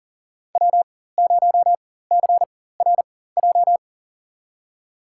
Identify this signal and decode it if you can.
Morse code, heard as a keyed tone, about 40 wpm, spaced as for 19 wpm, 700 Hz. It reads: W0CRJ
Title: USA Call Signs - 5 Characters - 4x Character Spacing - 40wpm